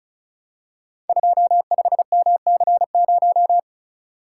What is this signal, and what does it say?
Morse code, heard as a keyed tone, about 36 wpm, 700 Hz. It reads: 25MC0
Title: Code Groups - 5 Letter:Numbers - 35wpm